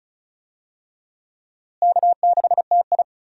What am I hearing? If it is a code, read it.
Morse code, heard as a keyed tone, about 34 wpm, 700 Hz. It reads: K6TI
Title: USA Call Signs - 4 Characters - Repeated 2x - 35wpm